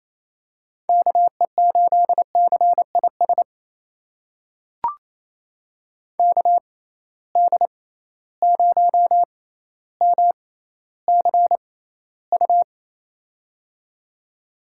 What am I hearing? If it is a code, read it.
Morse code, heard as a keyed tone, about 28 wpm, spaced as for 14 wpm, 700 Hz. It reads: KE8CIS KD0MCU
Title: USA Call Signs - 6 Characters - 6x Character Spacing - 28wpm